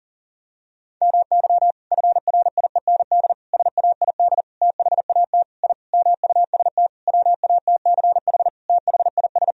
Morse code, heard as a keyed tone, about 40 wpm, 700 Hz. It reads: MY FRIEND SAID THAT I MUST WATCH THIS
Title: Sentences from Top 1000 Words - Review Part 4 - 40wpm